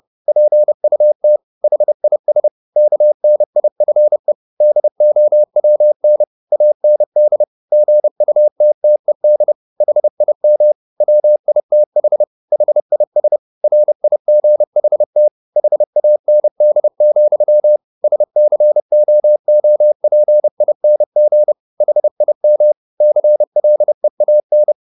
Morse code, heard as a keyed tone, about 30 wpm, 600 Hz. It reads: PUT HIS KNIFE DOWN AND GUTTED HIM WITH HIS RIGHT HAND, SCOOPING HIM CLEAN